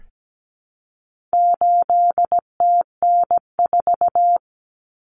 Morse code, heard as a keyed tone, about 17 wpm, 700 Hz. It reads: 8TN4
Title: Common Cut Numbers - 4-Digit 17wpm